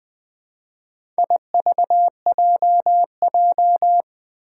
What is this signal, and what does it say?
Morse code, heard as a keyed tone, about 20 wpm, 700 Hz. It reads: IVJJ
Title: Four Challenging Letter-Number Combinations 20wpm